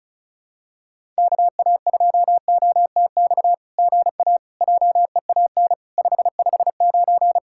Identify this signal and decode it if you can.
Morse code, heard as a keyed tone, about 33 wpm, 700 Hz. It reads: KA2OTX GA JEAN 559
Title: SOTA QSO Practice 35wpm